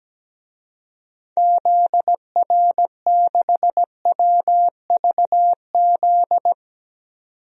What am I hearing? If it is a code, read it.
Morse code, heard as a keyed tone, about 17 wpm, 700 Hz. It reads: ZR6WVZ
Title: DXCC-100 Call Signs - 6 Characters - 17wpm